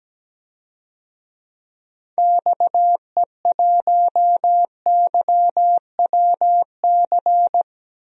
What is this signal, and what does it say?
Morse code, heard as a keyed tone, about 17 wpm, 700 Hz. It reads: XE1YWC
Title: DXCC-100 Call Signs - 6 Characters - Repeated 3x - 17wpm